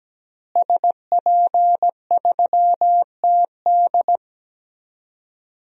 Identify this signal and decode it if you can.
Morse code, heard as a keyed tone, about 17 wpm, 700 Hz. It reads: SP3TD
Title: DXCC-100 Call Signs - 5 Characters - Repeated 3x - 17wpm